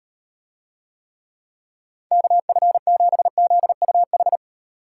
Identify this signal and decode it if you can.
Morse code, heard as a keyed tone, about 38 wpm, 700 Hz. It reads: KF7ZUH